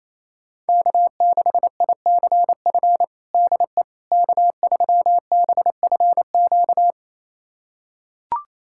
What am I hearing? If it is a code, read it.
Morse code, heard as a keyed tone, about 28 wpm, 700 Hz. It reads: K6ICF DE K3BFQ
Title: USA Callsign DE Callsign - 5 Characters - 28wpm